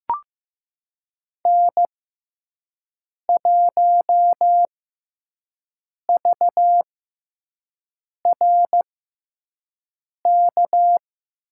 Morse code, heard as a keyed tone, about 15 wpm, spaced as for 5 wpm, 700 Hz. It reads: N1VRK
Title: USA Call Signs - 6x Character Spacing - 15wpm